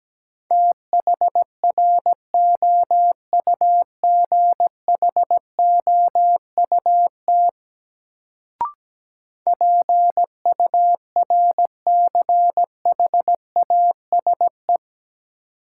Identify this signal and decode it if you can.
Morse code, heard as a keyed tone, about 17 wpm, 700 Hz. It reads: THROUGHOUT PURCHASE
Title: Sentences from Top 800 Words - New Word Drill - 17wpm